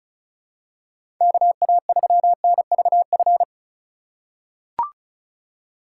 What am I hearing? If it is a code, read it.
Morse code, heard as a keyed tone, about 36 wpm, 700 Hz. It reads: KA3NVF
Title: USA Call Signs - 6 Characters - 3x Character Spacing - 35wpm